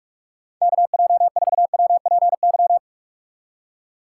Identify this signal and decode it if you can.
Morse code, heard as a keyed tone, about 46 wpm, 700 Hz. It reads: KJ4WPY
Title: USA Call Signs - 6 Characters - 6x Character Spacing - 45wpm